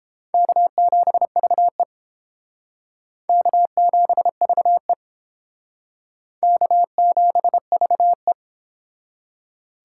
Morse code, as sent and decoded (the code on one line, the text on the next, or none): -.- --... ...- . / -.- --... ...- . / -.- --... ...- .
K7VE K7VE K7VE